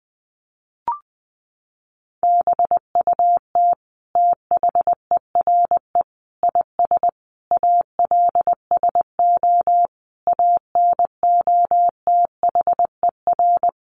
Morse code, heard as a keyed tone, about 19 wpm, 700 Hz. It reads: BUT THERE IS ALSO ANOTHER
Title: Sentences from Top 1500 Words 20wpm